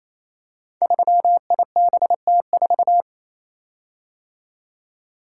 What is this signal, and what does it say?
Morse code, heard as a keyed tone, about 28 wpm, 700 Hz. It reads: 3IBT4
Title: Code Groups - 5 Letter:Numbers - 28wpm